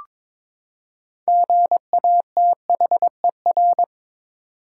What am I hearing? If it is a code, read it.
Morse code, heard as a keyed tone, about 22 wpm, 700 Hz. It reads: GATHER
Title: Six Letter Words - 22wpm